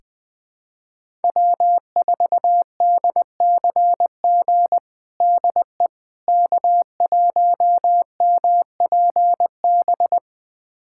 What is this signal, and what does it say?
Morse code, heard as a keyed tone, about 20 wpm, 700 Hz. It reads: W4DCG DE K1MPB
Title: USA Callsign DE Callsign - 5 Characters - 20wpm